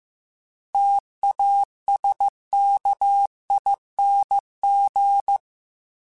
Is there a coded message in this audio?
Yes: Morse code from 0.7 to 5.4 s, heard as a keyed tone, about 15 wpm, 800 Hz. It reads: TASKING